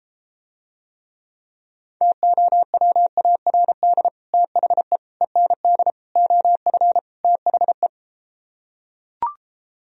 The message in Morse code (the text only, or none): TOWARD THE END OF THE